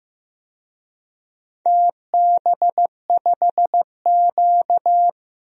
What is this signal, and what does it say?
Morse code, heard as a keyed tone, about 15 wpm, 700 Hz. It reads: TB5Q
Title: DXCC-100 Call Signs - 4 Characters - Repeated 3x - 15wpm